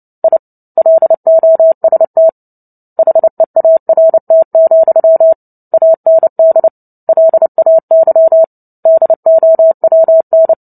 Morse code, heard as a keyed tone, about 30 wpm, 650 Hz. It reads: I LOST HEART, AND LAY DOWN
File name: Book Two / 03-Chapter 3 - 29 WPM